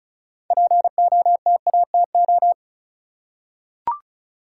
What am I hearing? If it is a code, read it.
Morse code, heard as a keyed tone, about 36 wpm, 700 Hz. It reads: POTATO
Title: Six Letter Words - 35wpm